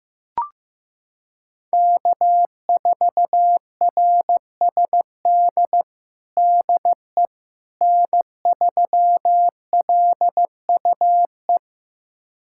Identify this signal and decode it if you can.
Morse code, heard as a keyed tone, about 15 wpm, 700 Hz. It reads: K4RSD DE N3LUE
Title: USA Callsign DE Callsign - 5 Characters - 15wpm